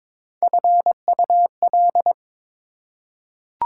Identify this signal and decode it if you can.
Morse code, heard as a keyed tone, about 22 wpm, 700 Hz. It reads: FUL
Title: Prefixes and Suffixes 22wpm